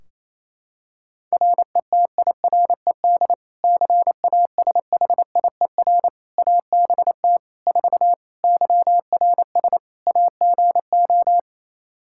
Morse code, heard as a keyed tone, about 28 wpm, 700 Hz. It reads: RETIRED CASHIER ABT 4 YRS AGO
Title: Practice QSO - Exchange 3 - More Info - 28wpm